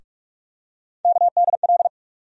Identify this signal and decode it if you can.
Morse code, heard as a keyed tone, about 46 wpm, 700 Hz. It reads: KDL
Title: USA Call Sign Postfixes - 3 Letters 45wpm